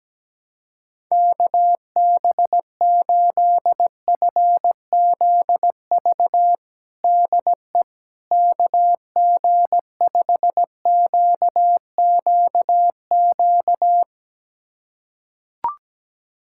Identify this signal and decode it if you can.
Morse code, heard as a keyed tone, about 17 wpm, 700 Hz. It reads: KB8FZV DE KG5QQQ
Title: USA Callsign DE Callsign - 6 Characters - 17wpm